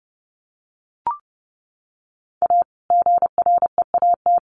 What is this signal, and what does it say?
Morse code, heard as a keyed tone, about 30 wpm, 700 Hz. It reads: A GREAT
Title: Sets of 2 Words 30wpm